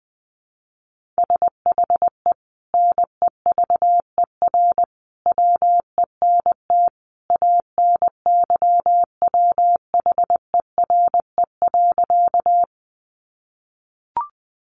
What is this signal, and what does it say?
Morse code, heard as a keyed tone, about 19 wpm, 700 Hz. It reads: SHE NEVER WENT ANYWHERE.
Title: Sentences from Top 1400 Words 20wpm